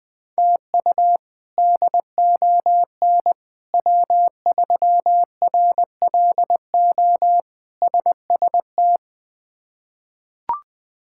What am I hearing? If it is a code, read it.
Morse code, heard as a keyed tone, about 20 wpm, 700 Hz. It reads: TU DON W3RLO SST
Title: K1USN SST QSO Practice - 20wpm